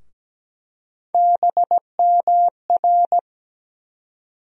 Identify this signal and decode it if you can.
Morse code, heard as a keyed tone, about 17 wpm, 700 Hz. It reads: BMR
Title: USA Call Sign Postfixes - 3 Letters 17wpm